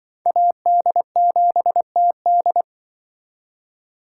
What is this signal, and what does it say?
Morse code, heard as a keyed tone, about 25 wpm, 700 Hz. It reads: AD7TD
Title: Speed Racing - USA 2x2 Call Signs 20wpm